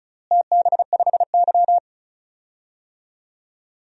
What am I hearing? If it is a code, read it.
Morse code, heard as a keyed tone, about 36 wpm, 700 Hz. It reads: TB5Y
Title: DXCC-100 Call Signs - 4 Characters - Repeated 3x - 35wpm